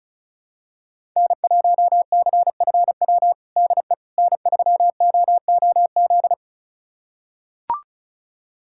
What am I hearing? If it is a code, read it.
Morse code, heard as a keyed tone, about 36 wpm, 700 Hz. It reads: N1CFW DE N3OOZ
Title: USA Callsign DE Callsign 35wpm